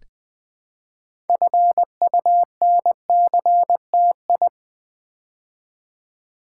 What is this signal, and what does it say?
Morse code, heard as a keyed tone, about 20 wpm, 700 Hz. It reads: FUNCTI
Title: Build A Word - 3 Letters At A Time - 8-Letter Words - 20wpm